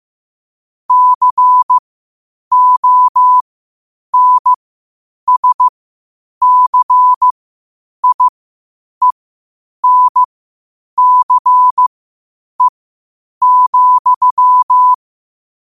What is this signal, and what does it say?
Morse code, heard as a keyed tone, about 15 wpm, spaced as for 8 wpm, 1000 Hz. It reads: CONSCIENCE,
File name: qotdfiles / 8wpm